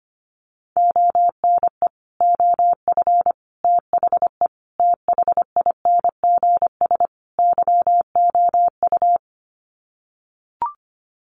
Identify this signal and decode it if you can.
Morse code, heard as a keyed tone, about 25 wpm, 700 Hz. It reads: ONE OF THE THINGS YOU
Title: Sets of 5 Words Encore2 25wpm